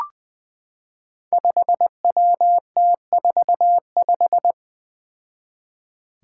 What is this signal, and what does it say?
Morse code, heard as a keyed tone, about 20 wpm, 700 Hz. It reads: 5WT45
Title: Five Challenging Letter-Number Combinations 20wpm